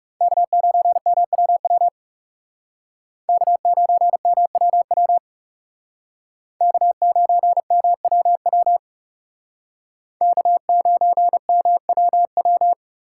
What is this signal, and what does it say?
Morse code, heard as a keyed tone, about 36 wpm, 700 Hz. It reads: K9MWW K9MWW K9MWW K9MWW
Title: Speed Racing - USA 1x3 Call Signs 30wpm